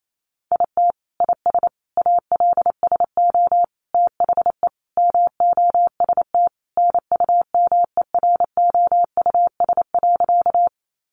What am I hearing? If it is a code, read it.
Morse code, heard as a keyed tone, about 28 wpm, 700 Hz. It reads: IT IS ALSO THE MOST NUMEROUS.
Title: Sentences from Top 1400 Words 28wpm